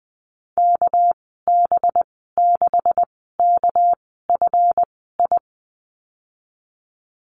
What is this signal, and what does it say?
Morse code, heard as a keyed tone, about 20 wpm, spaced as for 14 wpm, 700 Hz. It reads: KB6KFI
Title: 231 Lesson 37g - Letter x - Call Signs 20wpm